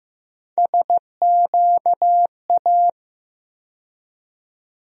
Morse code, heard as a keyed tone, about 15 wpm, 700 Hz. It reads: SQA
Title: USA Call Sign Postfixes - 3 Letters 15wpm